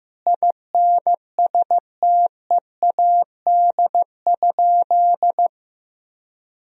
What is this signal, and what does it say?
Morse code, heard as a keyed tone, about 15 wpm, 700 Hz. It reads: INSTEAD?
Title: Sentences from Top 1000 Words - Review Part 4 - 15wpm